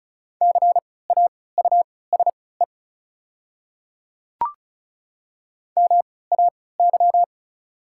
Morse code, heard as a keyed tone, about 35 wpm, spaced as for 20 wpm, 690 Hz. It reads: CAUSE MAY
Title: 3x Character Spacing - Top 200 Words - 35wpm